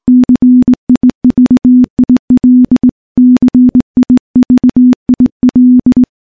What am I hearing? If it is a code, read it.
Morse code, heard as a keyed tone, about 20 wpm, 250 Hz. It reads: CIVIL CIVIL
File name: morse-training-20WPM